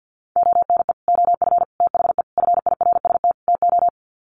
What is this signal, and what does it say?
Morse code, heard as a keyed tone, about 50 wpm, 700 Hz. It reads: ONE OF THE FIRST TO
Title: Sets of 5 Words Encore 50wpm